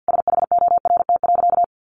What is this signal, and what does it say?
Morse code, heard as a keyed tone, about 50 wpm, 700 Hz. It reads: SHORT.
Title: Sentences from Top 500 Words 50wpm